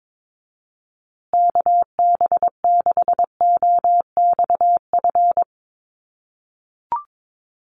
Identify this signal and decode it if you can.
Morse code, heard as a keyed tone, about 22 wpm, 700 Hz. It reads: KB6OXF
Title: USA Call Signs Repeated 2x 22wpm